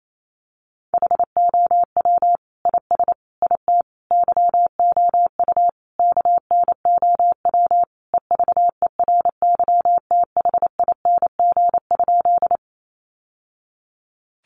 Morse code, heard as a keyed tone, about 28 wpm, 700 Hz. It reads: HOW IS IT YOU KNOW EVERYTHING?
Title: Sentences from Top 700 Words 28wpm